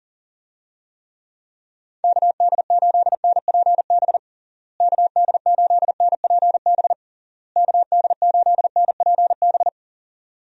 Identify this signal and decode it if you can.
Morse code, heard as a keyed tone, about 40 wpm, 700 Hz. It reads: KD8NPB KD8NPB KD8NPB
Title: USA Call Signs - 6 Characters - Repeated 3x - 40wpm